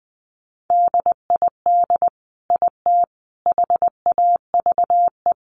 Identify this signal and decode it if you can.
Morse code, heard as a keyed tone, about 19 wpm, 700 Hz. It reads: DID IT HAVE
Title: Sentences from Top 1200 Words 20wpm